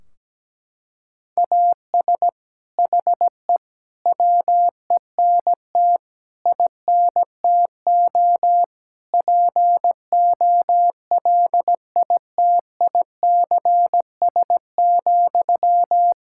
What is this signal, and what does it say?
Morse code, heard as a keyed tone, about 17 wpm, 700 Hz. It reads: AS HE WENT INTO POLITICS,
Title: Sentences from Top 1600 Words 17wpm